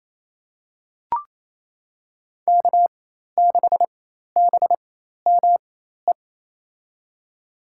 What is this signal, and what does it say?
Morse code, heard as a keyed tone, about 27 wpm, spaced as for 13 wpm, 700 Hz. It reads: K6BME
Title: USA Call Signs - 5 Characters - 4x Character Spacing - 28wpm